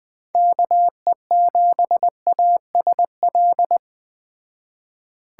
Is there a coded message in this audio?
Yes: Morse code from 0.3 to 3.8 s, heard as a keyed tone, about 20 wpm, 700 Hz. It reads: KE7ASL